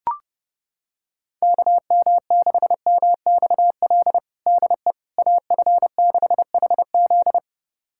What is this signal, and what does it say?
Morse code, heard as a keyed tone, about 30 wpm, 700 Hz. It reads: KM6MXL DE AF6HZ
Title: USA Callsign DE Callsign 30wpm